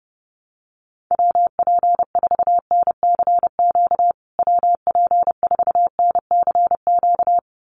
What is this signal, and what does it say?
Morse code, heard as a keyed tone, about 30 wpm, 700 Hz. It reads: WP4NCQ WP4NCQ
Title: USA Call Signs Repeated 2x 30wpm